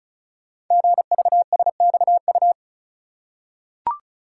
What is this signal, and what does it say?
Morse code, heard as a keyed tone, about 36 wpm, 700 Hz. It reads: GVSXU
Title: Code Groups - 5 Letters - 35wpm